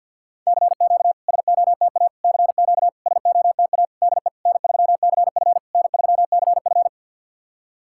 Morse code, heard as a keyed tone, about 51 wpm, 690 Hz. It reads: CQ SOTA CQ SOTA DE N3CF N3CF
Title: SOTA QSO Practice 50wpm